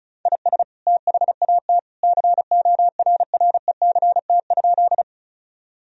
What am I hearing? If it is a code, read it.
Morse code, heard as a keyed tone, about 36 wpm, 700 Hz. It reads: IS THAT CORRECT?
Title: Sentences from Top 1000 Words - Review Part 2 - 35wpm